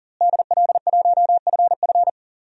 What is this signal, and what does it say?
Morse code, heard as a keyed tone, about 40 wpm, 700 Hz. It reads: DL1FF
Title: DXCC-100 Call Signs - 5 Characters - Repeated 2x - 40wpm